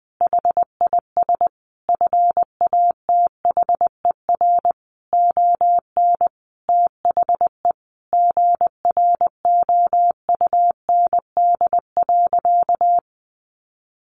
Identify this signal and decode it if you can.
Morse code, heard as a keyed tone, about 20 wpm, 700 Hz. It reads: HIS FATHER ON THE GROUND.
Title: Sentences from Top 500 Words 20wpm